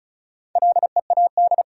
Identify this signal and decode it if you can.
Morse code, heard as a keyed tone, about 34 wpm, 700 Hz. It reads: LEAD
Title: Sentences from Top 500 Words - New Word Drill - 35wpm